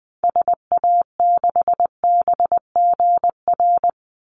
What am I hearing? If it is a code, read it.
Morse code, heard as a keyed tone, about 20 wpm, 700 Hz. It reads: SA6BGR
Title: Call Signs - 20wpm